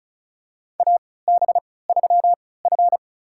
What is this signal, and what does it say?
Morse code, heard as a keyed tone, about 34 wpm, spaced as for 20 wpm, 700 Hz. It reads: AB3F
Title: USA Call Signs - 4 Characters - 3x Character Spacing - 35wpm